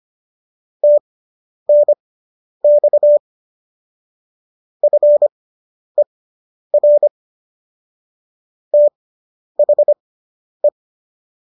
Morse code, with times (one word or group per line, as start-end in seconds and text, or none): 0.8-3.2 TNX
4.8-7.1 FER
8.7-10.7 THE